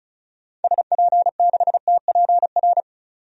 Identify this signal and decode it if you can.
Morse code, heard as a keyed tone, about 34 wpm, 700 Hz. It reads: SP6TPR
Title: DXCC-100 Call Signs - 6 Characters - 35wpm